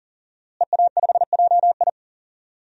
Morse code, heard as a keyed tone, about 40 wpm, 700 Hz. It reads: EA5JI
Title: DXCC-100 Call Signs - 5 Characters - Repeated 3x - 40wpm